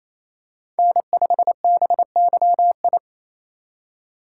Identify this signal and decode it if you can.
Morse code, heard as a keyed tone, about 28 wpm, 700 Hz. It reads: N5BYI
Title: USA Call Signs - 8x Character Spacing - 28wpm